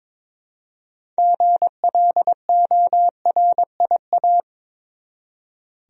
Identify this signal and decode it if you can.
Morse code, heard as a keyed tone, about 22 wpm, 700 Hz. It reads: GLORIA
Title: USA Common Names - 22wpm